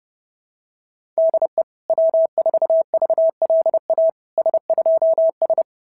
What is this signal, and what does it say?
Morse code, heard as a keyed tone, about 30 wpm, 650 Hz. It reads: DE W4VLA S2S
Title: SOTA QSO Practice 30wpm